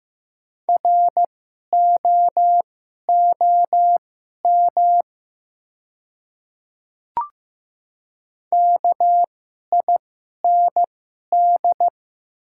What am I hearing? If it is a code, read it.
Morse code, heard as a keyed tone, about 15 wpm, spaced as for 11 wpm, 700 Hz. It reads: ROOM KIND
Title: 2x Character Spacing - Top 300 Words - 15wpm